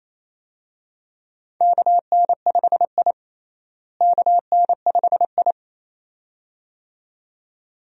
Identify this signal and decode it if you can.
Morse code, heard as a keyed tone, about 28 wpm, 700 Hz. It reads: KN5I KN5I
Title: USA Call Signs - 4 Characters - Repeated 2x - 28wpm